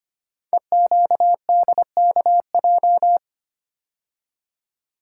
Morse code, heard as a keyed tone, about 25 wpm, 700 Hz. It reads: EQDKJ
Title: Code Groups - 5 Letters - 25wpm